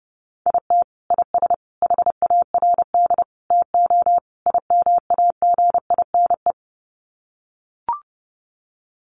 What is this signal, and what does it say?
Morse code, heard as a keyed tone, about 28 wpm, 700 Hz. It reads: IT IS HARD TO IMAGINE
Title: Sets of 5 Words Encore2 30wpm